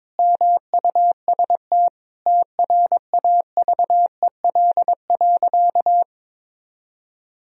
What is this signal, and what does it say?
Morse code, heard as a keyed tone, about 22 wpm, 700 Hz. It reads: MUST TRAVEL.
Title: Sentences from Top 1000 Words - Review Part 2 - 22wpm